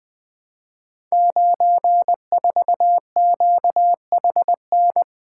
Morse code, heard as a keyed tone, about 20 wpm, 700 Hz. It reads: 94QHN
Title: Code Groups - 5 Letter:Numbers - 20wpm